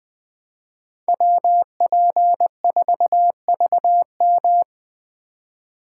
Morse code, heard as a keyed tone, about 20 wpm, 700 Hz. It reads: WP4VM
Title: Speed Racing - USA 2x2 Call Signs 17wpm